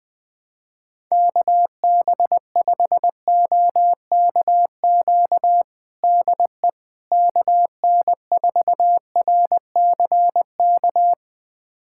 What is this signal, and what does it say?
Morse code, heard as a keyed tone, about 20 wpm, 700 Hz. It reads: KB5OKQ DE KN4RCK
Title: USA Callsign DE Callsign 20wpm